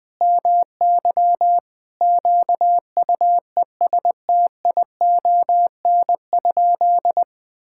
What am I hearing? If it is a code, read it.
Morse code, heard as a keyed tone, about 21 wpm, 700 Hz. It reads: MY QUESTION?
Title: Sentences from Top 1000 Words - Review Part 5 - 20wpm